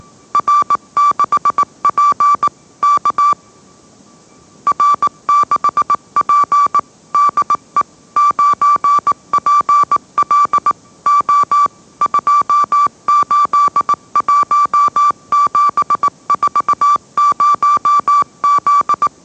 Morse code, heard as a keyed tone, about 18 wpm, 1160 Hz. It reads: R6P K R6P DE 9PL O 281740Z
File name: mrx / mrx081 20wpm